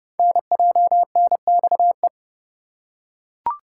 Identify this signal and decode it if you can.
Morse code, heard as a keyed tone, about 30 wpm, 700 Hz. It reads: NJNXE
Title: Code Groups - 5 Letters - 30wpm